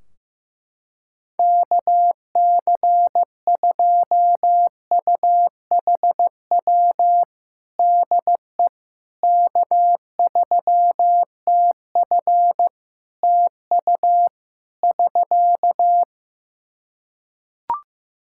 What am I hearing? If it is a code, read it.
Morse code, heard as a keyed tone, about 15 wpm, 700 Hz. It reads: KC2UHW DE K3TF TU <SK>
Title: Practice QSO - Exchange 4 - Ending - 15wpm